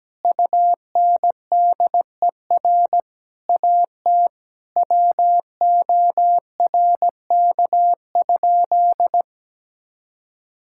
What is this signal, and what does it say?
Morse code, heard as a keyed tone, about 17 wpm, 700 Hz. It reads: UNDER AT WORK?
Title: Sentences from Top 1000 Words - Review Part 6 - 17wpm